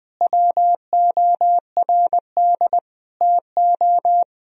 Morse code, heard as a keyed tone, about 21 wpm, 700 Hz. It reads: WORD TO
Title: Sentences from Top 1000 Words - Review Part 4 - 20wpm